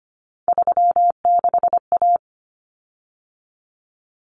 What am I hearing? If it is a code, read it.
Morse code, heard as a keyed tone, about 25 wpm, 700 Hz. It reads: 36A
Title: Common Cut Numbers - 3-Digit 25wpm